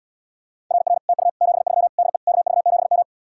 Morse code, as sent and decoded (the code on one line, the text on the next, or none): .-. .-. / - ..- / --... ...-- / -.. / -.- ...- --... --
RR TU 73 DE KV7M